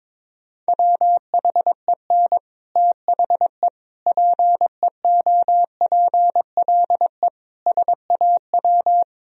Morse code, heard as a keyed tone, about 22 wpm, 700 Hz. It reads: WHEN THE PEOPLE SAW
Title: Sentences from Top 1000 Words - Review Part 1 - 22wpm